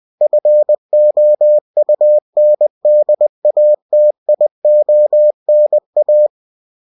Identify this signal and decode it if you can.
Morse code, heard as a keyed tone, about 20 wpm, 600 Hz. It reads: FOUNDATIONA